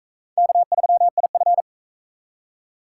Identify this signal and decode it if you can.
Morse code, heard as a keyed tone, about 38 wpm, 700 Hz. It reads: K3IF